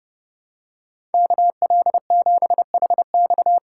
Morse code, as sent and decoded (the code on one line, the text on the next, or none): -.- .-.. --... .... -..-
KL7HX